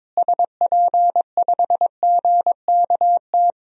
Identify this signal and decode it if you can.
Morse code, heard as a keyed tone, about 22 wpm, 700 Hz. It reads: SP5GKT
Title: DXCC-100 Call Signs - 6 Characters - Repeated 3x - 22wpm